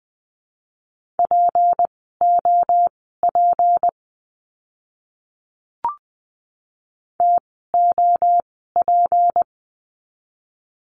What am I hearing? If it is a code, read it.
Morse code, heard as a keyed tone, about 20 wpm, spaced as for 14 wpm, 700 Hz. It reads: POP TOP
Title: 153 Lesson 25f - Letter p - Words 20wpm